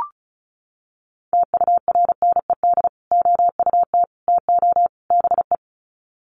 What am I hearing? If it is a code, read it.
Morse code, heard as a keyed tone, about 34 wpm, 700 Hz. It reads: TURNED OUT TO BE
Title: Sets of 4 Words 35wpm